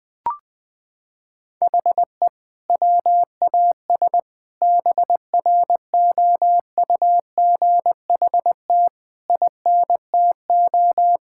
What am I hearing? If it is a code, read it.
Morse code, heard as a keyed tone, about 20 wpm, 700 Hz. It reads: HE WAS BROUGHT INTO
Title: Sentences from Top 1600 Words 20wpm